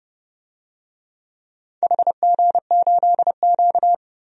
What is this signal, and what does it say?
Morse code, heard as a keyed tone, about 28 wpm, 700 Hz. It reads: HG8Q
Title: DXCC-100 Call Signs - 4 Characters - 30wpm